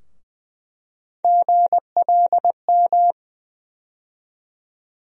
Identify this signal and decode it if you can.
Morse code, heard as a keyed tone, about 20 wpm, 700 Hz. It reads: GLM
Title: USA Call Sign Postfixes - 3 Letters 20wpm